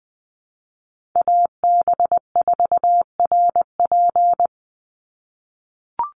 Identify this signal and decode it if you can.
Morse code, heard as a keyed tone, about 20 wpm, 700 Hz. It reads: AB4RP
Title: USA Call Signs 20wpm